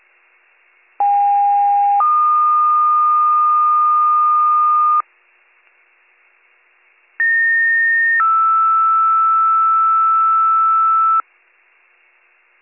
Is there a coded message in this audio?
Yes: two-tone page from 1.0 to 11.2 s, heard as a one-second tone, then a three-second tone, about 810 then 1220 Hz, about 1850 then 1320 Hz.